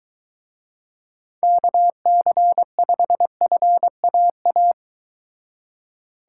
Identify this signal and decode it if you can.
Morse code, heard as a keyed tone, about 23 wpm, 700 Hz. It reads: KC5FAA